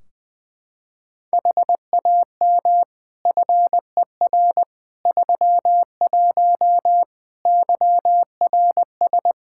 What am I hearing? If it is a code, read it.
Morse code, heard as a keyed tone, about 20 wpm, 700 Hz. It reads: HAM FER 31 YRS
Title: Practice QSO - Exchange 3 - More Info - 20wpm